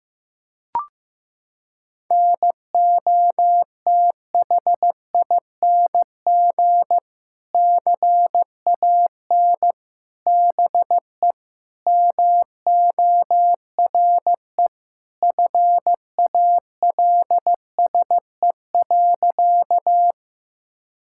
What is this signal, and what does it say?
Morse code, heard as a keyed tone, about 15 wpm, 700 Hz. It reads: NOTHING CAN BE MORE FALSE.
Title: Sentences from Top 1600 Words 15wpm